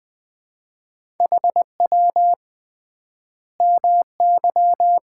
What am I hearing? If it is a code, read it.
Morse code, heard as a keyed tone, about 20 wpm, 700 Hz. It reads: HW MY
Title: Common QSO Phrases - 3x Word-Spacing - 20wpm